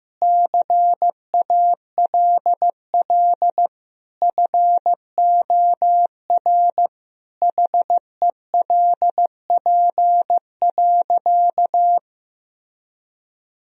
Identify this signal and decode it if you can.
Morse code, heard as a keyed tone, about 16 wpm, 700 Hz. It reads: CALL FOR HELP.
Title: Sentences from Top 1000 Words - Review Part 4 - 15wpm